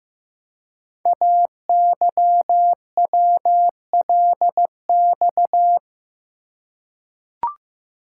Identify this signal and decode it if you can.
Morse code, heard as a keyed tone, about 15 wpm, 700 Hz. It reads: AYWLX
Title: Code Groups - 5 Letters - 15wpm